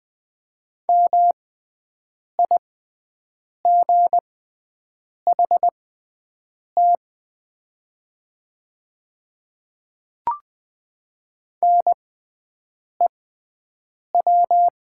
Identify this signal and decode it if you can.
Morse code, heard as a keyed tone, about 20 wpm, spaced as for 7 wpm, 700 Hz. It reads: MIGHT NEW
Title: 6x Character Spacing - Top 300 Words - 20wpm